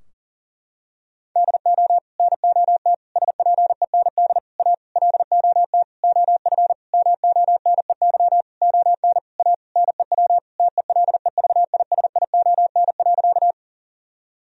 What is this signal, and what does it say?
Morse code, heard as a keyed tone, about 40 wpm, 700 Hz. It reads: DO NOT SPEND A LOT OF MONEY ON A NEW TELEVISION.
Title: Sentences from Top 1000 Words - Review Part 4 - 40wpm